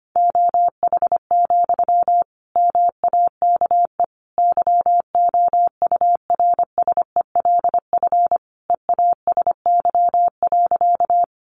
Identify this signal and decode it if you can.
Morse code, heard as a keyed tone, about 25 wpm, 700 Hz. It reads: OH, MAKE YOURSELF EASY.
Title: Sentences from Top 700 Words 25wpm